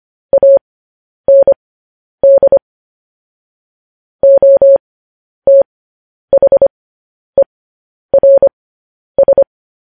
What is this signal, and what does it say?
Morse code, heard as a keyed tone, about 25 wpm, spaced as for 10 wpm, 550 Hz. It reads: AND OTHERS
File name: podcast / News headlines-1x-25x10